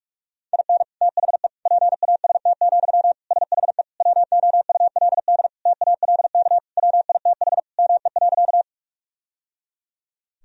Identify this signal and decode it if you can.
Morse code, heard as a keyed tone, about 46 wpm, 700 Hz. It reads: IN THE PAST, SHE WOULD TALK WITH ME.
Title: Sentences from Top 1000 Words - Review Part 3 - 45wpm